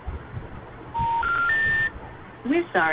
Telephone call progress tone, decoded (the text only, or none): special information tone